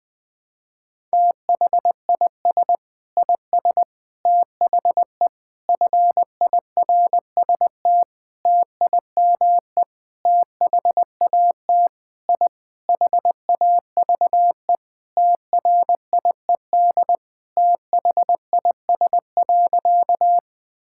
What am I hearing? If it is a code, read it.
Morse code, heard as a keyed tone, about 21 wpm, 700 Hz. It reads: THIS IS THE FIRST TIME THAT I HAVE TRIED THIS.
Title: Sentences with 6-Word Phrases - 20wpm